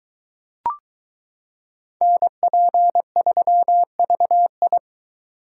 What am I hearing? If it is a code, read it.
Morse code, heard as a keyed tone, about 22 wpm, 700 Hz. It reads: NP3VI